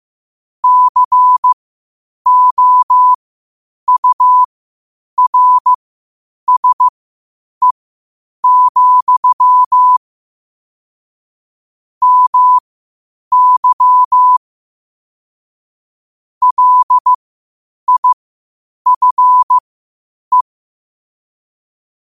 Morse code, heard as a keyed tone, about 15 wpm, spaced as for 8 wpm, 1000 Hz. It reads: COURSE, MY LIFE
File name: qotdfiles / 8wpm